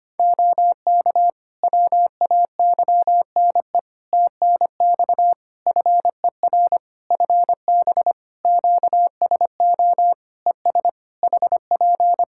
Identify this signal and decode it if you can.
Morse code, heard as a keyed tone, about 25 wpm, 700 Hz. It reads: OK WAYNE TNX FER FB QSO ES HP
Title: Practice QSO - Exchange 4 - Ending - 25wpm